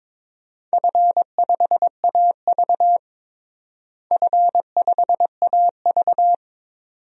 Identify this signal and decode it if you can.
Morse code, heard as a keyed tone, about 22 wpm, 700 Hz. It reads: F5AV F5AV
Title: DXCC-100 Call Signs - 4 Characters - Repeated 2x - 22wpm